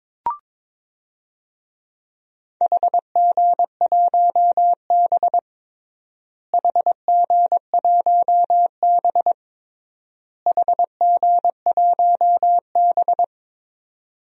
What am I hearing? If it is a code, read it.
Morse code, heard as a keyed tone, about 22 wpm, 700 Hz. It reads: HG1B HG1B HG1B
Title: DXCC-100 Call Signs - 4 Characters - Repeated 3x - 22wpm